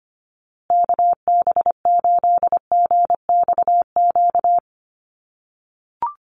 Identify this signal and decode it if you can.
Morse code, heard as a keyed tone, about 25 wpm, 700 Hz. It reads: KB8GXQ
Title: USA Call Signs Repeated 2x 25wpm